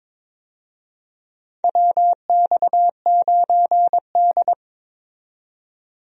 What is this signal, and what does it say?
Morse code, heard as a keyed tone, about 22 wpm, 700 Hz. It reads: WX9D
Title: USA Call Signs - 4 Characters - 22wpm